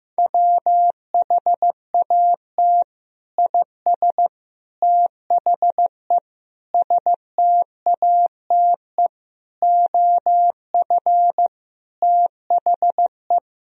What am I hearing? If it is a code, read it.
Morse code, heard as a keyed tone, about 15 wpm, 700 Hz. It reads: WHAT IS THE STATE OF THE
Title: Sentences from Top 1000 Words - Review Part 3 - 15wpm